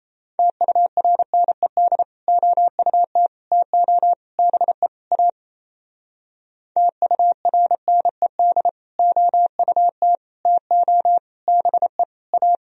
Morse code, heard as a keyed tone, about 36 wpm, 700 Hz. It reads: TURNED OUT TO BE A TURNED OUT TO BE A